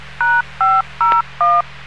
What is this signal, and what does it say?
DTMF, heard as telephone touch tones, tones of about 230 ms. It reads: #501